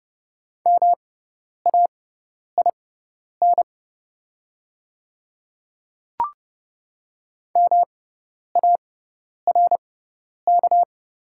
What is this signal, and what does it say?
Morse code, heard as a keyed tone, about 30 wpm, spaced as for 10 wpm, 700 Hz. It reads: MAIN MARK
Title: 6x Character Spacing - Top 300 Words - 30wpm